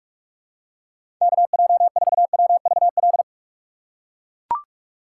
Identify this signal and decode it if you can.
Morse code, heard as a keyed tone, about 41 wpm, 700 Hz. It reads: KJ4WVL
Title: USA Call Signs - 6 Characters - 2x Character Spacing - 45wpm